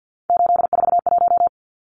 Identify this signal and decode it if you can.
Morse code, heard as a keyed tone, about 50 wpm, 700 Hz. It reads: 841
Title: Numbers - 3-Digit 50wpm